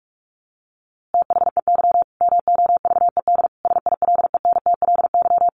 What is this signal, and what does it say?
Morse code, heard as a keyed tone, about 45 wpm, 700 Hz. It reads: THEY MOVED SILENTLY
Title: A Sound Of Thunder 45wpm